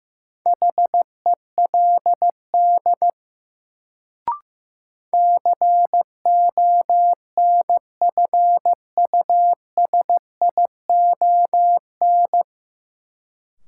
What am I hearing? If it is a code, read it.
Morse code, heard as a keyed tone, about 15 wpm, 700 Hz. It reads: HELD CONFUSION
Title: Sentences from Top 1600 Words - New Word Drill - 15wpm